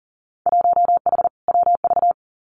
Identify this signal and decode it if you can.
Morse code, heard as a keyed tone, about 42 wpm, 700 Hz. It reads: 1H WV
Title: Winter Field Day Exchanges 40wpm